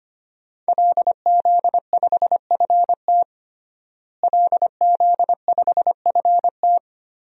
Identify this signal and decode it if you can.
Morse code, heard as a keyed tone, about 25 wpm, 700 Hz. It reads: LZ5FT LZ5FT
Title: DXCC-100 Call Signs - 5 Characters - Repeated 2x - 25wpm